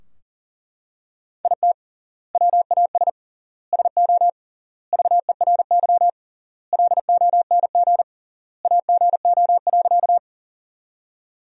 Morse code, heard as a keyed tone, about 40 wpm, 700 Hz. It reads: IT WAS SO VERY LONG AGO.
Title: 3x Word-Spacing - Sentences from Top 500 Words - 40wpm